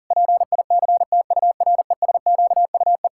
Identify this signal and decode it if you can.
Morse code, heard as a keyed tone, about 37 wpm, 700 Hz. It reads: PICTURESQUE
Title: Pangram Challenge v2 - 40wpm